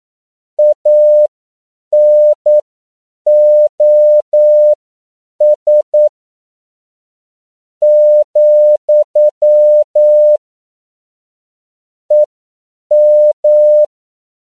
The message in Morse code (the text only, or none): ANOS , EM